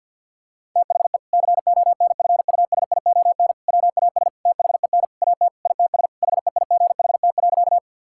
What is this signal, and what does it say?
Morse code, heard as a keyed tone, about 50 wpm, 700 Hz. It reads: THE CONFUSION WAS THEN AT ITS HEIGHT.
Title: Sentences from Top 1600 Words 50wpm